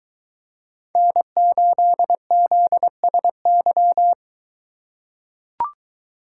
Morse code, heard as a keyed tone, about 22 wpm, 700 Hz. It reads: N8ZSY